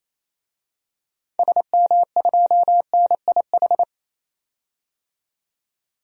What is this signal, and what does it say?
Morse code, heard as a keyed tone, about 28 wpm, 700 Hz. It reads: SM2NIH
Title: DXCC-100 Call Signs - 6 Characters - 28wpm